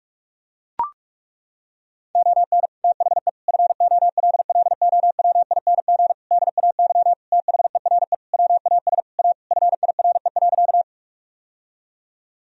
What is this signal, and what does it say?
Morse code, heard as a keyed tone, about 43 wpm, 700 Hz. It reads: ON THE FOLLOWING DAY THERE WAS A FIRE.
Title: Sentences from Top 1000 Words - Review Part 4 - 45wpm